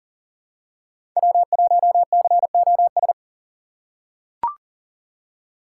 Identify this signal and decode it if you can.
Morse code, heard as a keyed tone, about 38 wpm, 700 Hz. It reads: W1COS
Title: USA Call Signs - 5 Characters - 3x Character Spacing - 40wpm